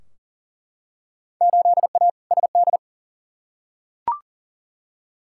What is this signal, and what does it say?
Morse code, heard as a keyed tone, about 38 wpm, 700 Hz. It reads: 8A SD
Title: Field Day Exchanges Encore - 40wpm